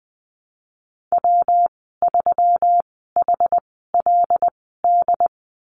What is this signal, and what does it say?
Morse code, heard as a keyed tone, about 20 wpm, spaced as for 15 wpm, 700 Hz. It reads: W3HLD
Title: 122 Lesson 20g - Number 3 - Call Signs 20wpm